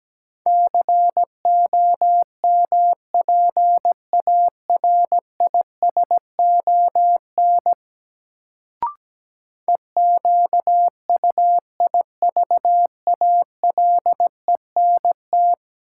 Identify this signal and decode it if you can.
Morse code, heard as a keyed tone, about 17 wpm, 700 Hz. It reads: COMPARISON EQUIVALENT
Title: Sentences from Top 1600 Words - New Word Drill Alt - 17wpm